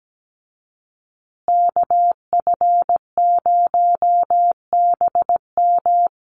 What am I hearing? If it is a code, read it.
Morse code, heard as a keyed tone, about 17 wpm, 700 Hz. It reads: KF0BM
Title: USA Call Signs Repeated 2x 17wpm